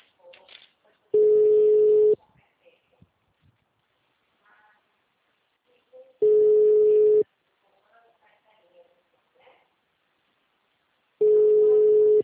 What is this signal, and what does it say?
Ringback tone (Europe, 425 Hz).